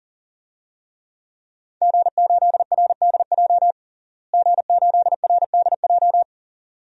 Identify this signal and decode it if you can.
Morse code, heard as a keyed tone, about 40 wpm, 700 Hz. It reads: G8RDJ G8RDJ
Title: DXCC-100 Call Signs - 5 Characters - Repeated 2x - 40wpm